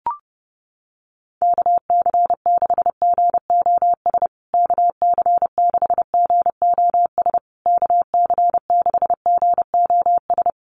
Morse code, heard as a keyed tone, about 30 wpm, 700 Hz. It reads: KC6GOS KC6GOS KC6GOS
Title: USA Call Signs Repeated 3x 30wpm